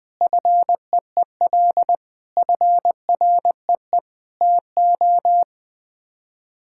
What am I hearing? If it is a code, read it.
Morse code, heard as a keyed tone, about 20 wpm, 700 Hz. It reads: FEEL FREE TO
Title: Sets of 8 Words - 20wpm